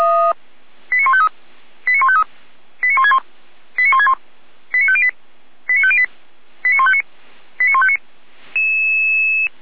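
Five-tone selective call (ZVEI), heard as quick runs of five tones, tones of about 70 ms. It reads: 89153 89153 89161 89161 89508 89508 89159 89159
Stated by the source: ZVEI five-tone call 89160 received on an SDR, recorded by feuerwehr-orlamuende (fire brigade Orlamünde).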